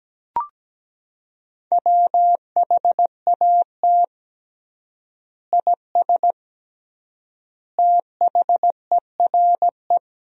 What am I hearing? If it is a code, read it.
Morse code, heard as a keyed tone, about 17 wpm, 700 Hz. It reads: WHAT IS THERE
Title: 3x Word-Spacing - Sentences from Top 100 Words - 17wpm